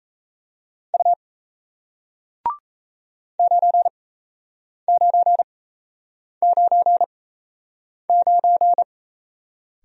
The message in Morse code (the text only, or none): U 9999